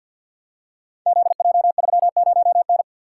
Morse code, heard as a keyed tone, about 51 wpm, 700 Hz. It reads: ZJ30N
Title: Code Groups - 5 Letter:Numbers - 50wpm